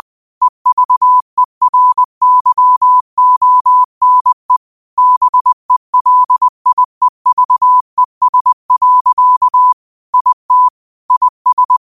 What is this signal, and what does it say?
Morse code, heard as a keyed tone, about 20 wpm, 1000 Hz. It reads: EVERYONE BELIEVES. IT IS
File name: 20wpm